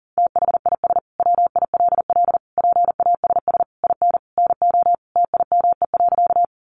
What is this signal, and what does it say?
Morse code, heard as a keyed tone, about 37 wpm, 700 Hz. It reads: THIS WILL PASS IN NO TIME.
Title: Sentences from Top 400 Words 40wpm